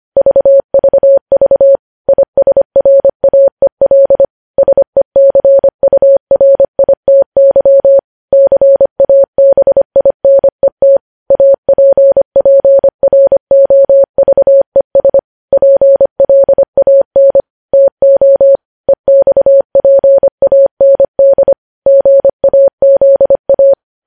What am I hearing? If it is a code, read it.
Morse code, heard as a keyed tone, about 25 wpm, 550 Hz. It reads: VVV ISRAEL SECURITY CABINET APPROVES PLAN TO EXPAND GAZA
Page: News Headlines in Morse Code at 25 WPM